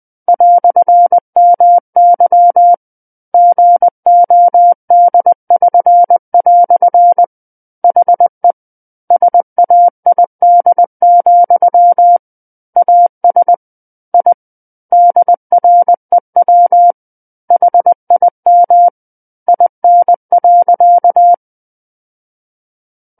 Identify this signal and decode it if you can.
Morse code, heard as a keyed tone, about 20 wpm, 700 Hz. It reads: "MY GOD<SN>" HE SAID, AS I DREW HIM IN.
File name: 20 WPM